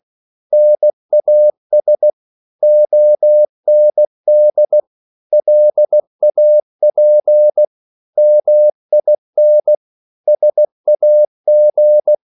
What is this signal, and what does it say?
Morse code, heard as a keyed tone, about 16 wpm, 600 Hz. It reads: NAS OND LAP MIN SAG
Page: Korte danske ord 16wpm